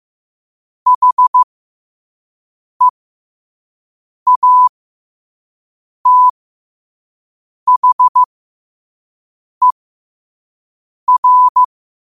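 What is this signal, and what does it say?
Morse code, heard as a keyed tone, about 15 wpm, spaced as for 5 wpm, 1000 Hz. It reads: HEATHER
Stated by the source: Morse code practice at 5 wpm.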